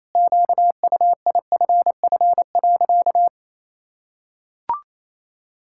Morse code, heard as a keyed tone, about 27 wpm, 700 Hz. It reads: QUIFF.
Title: Pangram Challenge v2 - 28wpm